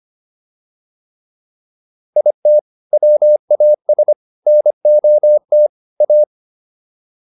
Morse code, heard as a keyed tone, about 25 wpm, 600 Hz. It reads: IT WAS NOT A